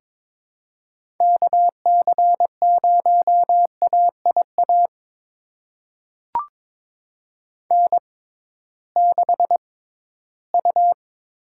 Morse code, heard as a keyed tone, about 22 wpm, spaced as for 14 wpm, 700 Hz. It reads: KC0AIA N6U
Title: USA Call Signs - 6x Character Spacing - 22wpm